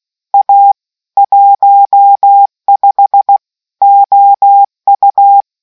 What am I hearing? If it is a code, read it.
Morse code, heard as a keyed tone, about 16 wpm, 800 Hz. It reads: A 15 OU